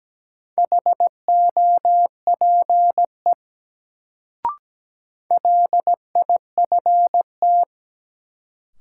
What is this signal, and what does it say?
Morse code, heard as a keyed tone, about 17 wpm, 700 Hz. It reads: HOPE LIFT
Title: Sentences from Top 1000 Words - New Word Drill - 17wpm